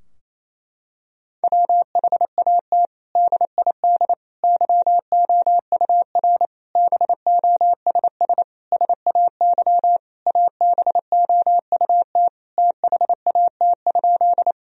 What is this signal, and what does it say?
Morse code, heard as a keyed tone, about 27 wpm, 700 Hz. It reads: WHAT DID YOUR BOSS SAY ABOUT THAT?
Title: Sentences from Top 1000 Words - Review Part 4 - 28wpm